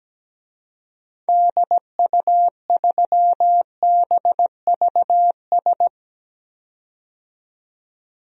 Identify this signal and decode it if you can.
Morse code, heard as a keyed tone, about 17 wpm, 700 Hz. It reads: DU3BVS
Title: DXCC-100 Call Signs - 6 Characters - 17wpm